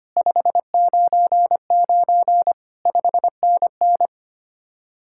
Morse code, heard as a keyed tone, about 31 wpm, 700 Hz. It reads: 599 5NN
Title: Speed-Racing Common QSO Phrases - 25wpm